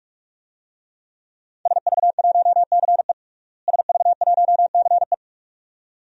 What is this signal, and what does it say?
Morse code, heard as a keyed tone, about 46 wpm, 700 Hz. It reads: SV1CE SV1CE
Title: DXCC-100 Call Signs - 5 Characters - Repeated 2x - 45wpm